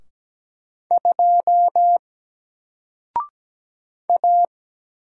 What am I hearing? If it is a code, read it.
Morse code, heard as a keyed tone, about 17 wpm, 700 Hz. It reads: 2 A
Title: Single Letter-Number Alt 17wpm